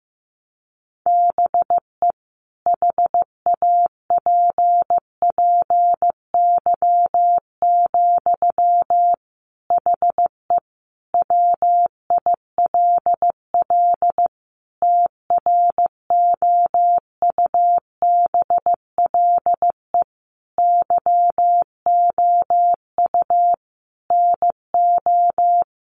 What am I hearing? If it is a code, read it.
Morse code, heard as a keyed tone, about 15 wpm, 700 Hz. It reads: BE HAPPY, HE WILL TROUBLE YOU NO
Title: Sentences from Top 1200 Words 15wpm